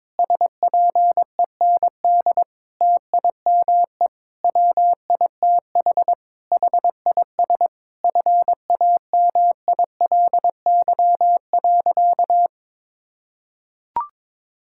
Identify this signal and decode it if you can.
Morse code, heard as a keyed tone, about 22 wpm, 700 Hz. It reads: SPEND TIME WITH HIS FAMILY.
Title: Sentences from Top 1000 Words - Review Part 6 - 22wpm